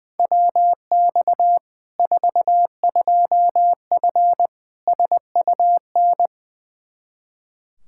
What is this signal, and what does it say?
Morse code, heard as a keyed tone, about 20 wpm, 700 Hz. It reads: WX 42F SUN
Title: K1USN SST QSO Practice 20wpm